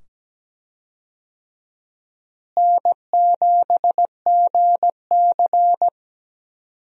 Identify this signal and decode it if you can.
Morse code, heard as a keyed tone, about 17 wpm, 700 Hz. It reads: N7GC
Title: USA Call Signs - 4 Characters - Repeated 2x - 17wpm